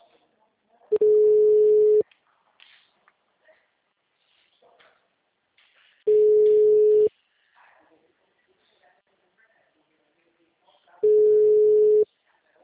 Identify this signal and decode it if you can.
Ringback tone (Europe, 425 Hz), with speech.
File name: out- -2013- - - .13119